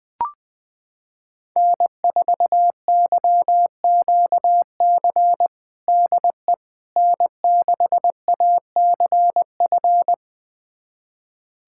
Morse code, heard as a keyed tone, about 21 wpm, 700 Hz. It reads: N4YQC DE N6ACF
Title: USA Callsign DE Callsign - 5 Characters - 20wpm